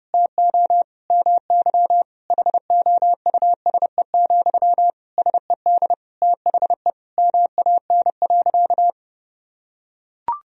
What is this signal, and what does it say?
Morse code, heard as a keyed tone, about 31 wpm, 700 Hz. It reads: TO MY HOUSE, SED THE MAN.
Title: Sentences from Top 400 Words and Abbreviations 30wpm